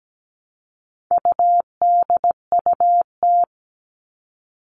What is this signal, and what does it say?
Morse code, heard as a keyed tone, about 17 wpm, 700 Hz. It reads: UDUT
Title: All Cut Numbers - 4-Digit 17wpm